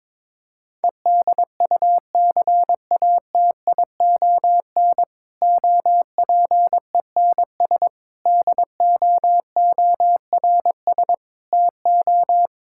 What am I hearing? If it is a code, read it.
Morse code, heard as a keyed tone, about 22 wpm, 700 Hz. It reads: EDUCATION OPENS DOORS TO
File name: Difficult Sentences with Hints - 22wpm